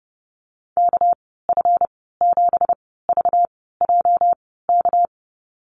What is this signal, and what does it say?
Morse code, heard as a keyed tone, about 30 wpm, spaced as for 17 wpm, 700 Hz. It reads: KF7VJK
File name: Lesson 37g - Letter x - Call Signs 30wpm